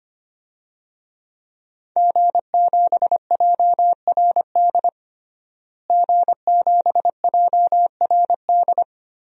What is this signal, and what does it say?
Morse code, heard as a keyed tone, about 25 wpm, 700 Hz. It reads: G7JRD G7JRD
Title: DXCC-100 Call Signs - 5 Characters - Repeated 3x - 25wpm